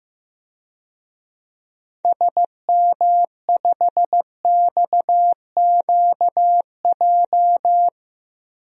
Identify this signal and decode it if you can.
Morse code, heard as a keyed tone, about 15 wpm, 700 Hz. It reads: SM5XQJ
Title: DXCC-100 Call Signs - 6 Characters - 15wpm